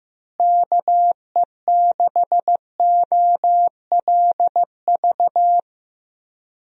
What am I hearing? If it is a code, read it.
Morse code, heard as a keyed tone, about 15 wpm, 700 Hz. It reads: KE6OLV
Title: USA Call Signs - 6 Characters - Repeated 3x - 15wpm